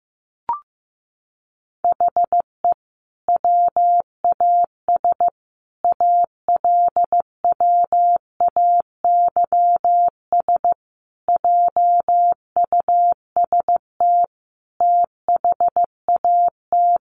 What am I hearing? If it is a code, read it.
Morse code, heard as a keyed tone, about 15 wpm, 700 Hz. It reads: HE WAS ALWAYS JUST THAT
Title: Sentences from Top 600 Words 15wpm